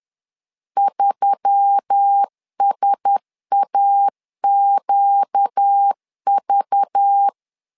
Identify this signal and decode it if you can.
Morse code, heard as a keyed tone, about 10 wpm, 800 Hz. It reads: 3SAQV
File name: LEZIONI / 5 LBVC / 5 LBV 10wpm 2248